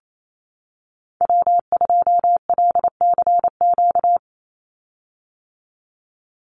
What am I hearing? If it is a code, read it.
Morse code, heard as a keyed tone, about 28 wpm, 700 Hz. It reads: W2LCQ
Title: Call Signs - 28wpm